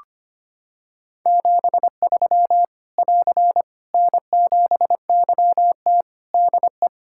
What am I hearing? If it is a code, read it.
Morse code, heard as a keyed tone, about 25 wpm, 700 Hz. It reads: 73 <AR> N7YT DE
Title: Practice QSO - Exchange 4 - Ending - 25wpm